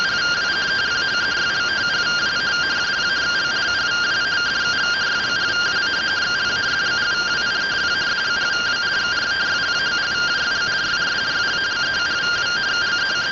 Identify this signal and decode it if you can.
Two-tone teleprinter signal, about 170 Hz shift.